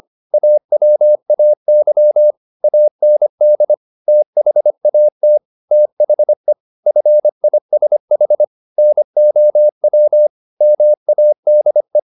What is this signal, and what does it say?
Morse code, heard as a keyed tone, about 25 wpm, 600 Hz. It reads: AWAY AND THAT THE FISH NOW MADE